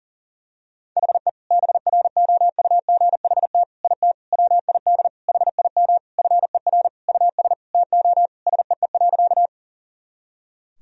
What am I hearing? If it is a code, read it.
Morse code, heard as a keyed tone, about 40 wpm, 700 Hz. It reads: HE BROUGHT IT WID HIM FER US TO SEE.
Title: Sentences from Top 500 Words and Abbreviations 40wpm